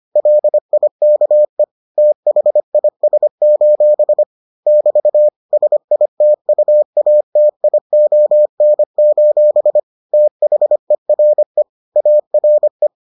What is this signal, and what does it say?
Morse code, heard as a keyed tone, about 25 wpm, 600 Hz. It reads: LIKE THIS: <BT> SITUATION: THERE ARE